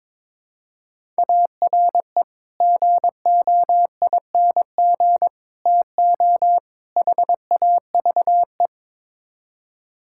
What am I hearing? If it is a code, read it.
Morse code, heard as a keyed tone, about 22 wpm, 700 Hz. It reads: ARE GOING TO HAVE
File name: practice / 22wpm / Build A Phrase - 5-Word Phrases - 22wpm